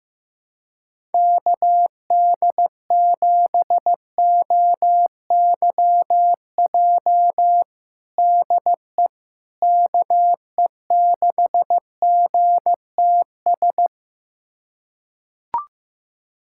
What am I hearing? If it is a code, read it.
Morse code, heard as a keyed tone, about 15 wpm, 700 Hz. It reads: KD7OYJ DE KE6GTS
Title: USA Callsign DE Callsign - 6 Characters - 15wpm